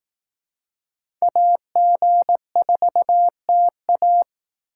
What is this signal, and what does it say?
Morse code, heard as a keyed tone, about 18 wpm, 700 Hz. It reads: AG4TA
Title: Speed Racing - USA 2x2 Call Signs 15wpm